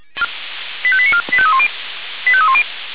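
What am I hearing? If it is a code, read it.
Five-tone selective call (ZVEI), heard as quick runs of five tones, tones of about 70 ms. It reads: 85894 85310 85310
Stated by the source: ZVEI five-tone call 85320 received by SDR.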